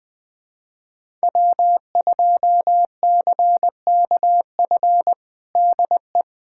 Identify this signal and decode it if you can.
Morse code, heard as a keyed tone, about 21 wpm, 700 Hz. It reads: W2CKF DE
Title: USA Callsign DE Callsign - 5 Characters - 20wpm